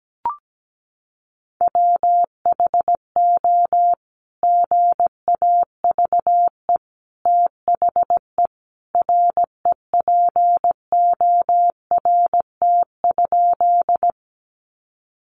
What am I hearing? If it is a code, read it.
Morse code, heard as a keyed tone, about 17 wpm, 700 Hz. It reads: WHO GAVE THE REPORT?
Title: Sentences from Top 900 Words 17wpm